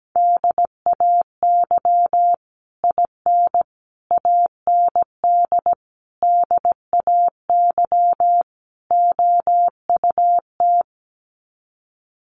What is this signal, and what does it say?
Morse code, heard as a keyed tone, about 17 wpm, 700 Hz. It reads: DAY IN AND DAY OUT
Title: Sets of 5 Words Encore2 17wpm